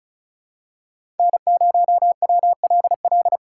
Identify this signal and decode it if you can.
Morse code, heard as a keyed tone, about 36 wpm, 700 Hz. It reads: N0WLL
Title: USA Call Signs - 5 Characters - 4x Character Spacing - 35wpm